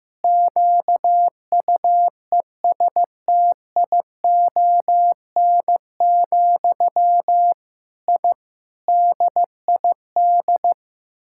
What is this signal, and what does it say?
Morse code, heard as a keyed tone, about 15 wpm, 700 Hz. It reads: QUESTION, I DID
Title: Sentences from Top 1000 Words - Review Part 6 - 15wpm